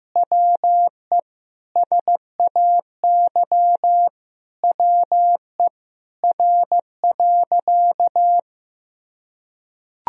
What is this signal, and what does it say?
Morse code, heard as a keyed tone, about 15 wpm, 700 Hz. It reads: WE SAY WE R.
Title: Sentences from Top 200 Words and Abbreviations 15wpm